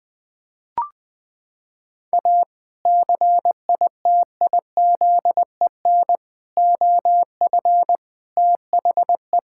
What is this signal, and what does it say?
Morse code, heard as a keyed tone, about 20 wpm, 700 Hz. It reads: A CITIZEN OF THE
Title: Sentences with 3-Word Phrases - 20wpm